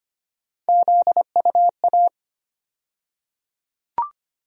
Morse code, heard as a keyed tone, about 25 wpm, 700 Hz. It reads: ZUA
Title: USA Call Sign Postfixes - 3 Letters 25wpm